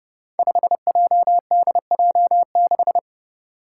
Morse code, heard as a keyed tone, about 30 wpm, 700 Hz. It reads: 5JDJ6
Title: Five Challenging Letter-Number Combinations 30wpm